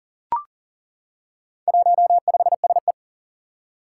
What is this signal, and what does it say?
Morse code, heard as a keyed tone, about 40 wpm, 700 Hz. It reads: 15SE
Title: Four Challenging Letter-Number Combinations 40wpm